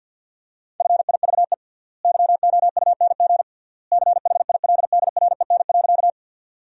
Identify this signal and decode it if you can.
Morse code, heard as a keyed tone, about 52 wpm, 700 Hz. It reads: FIVE YOUNG CHILDREN.
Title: 3x Word-Spacing - Sentences from Top 400 Words - 50wpm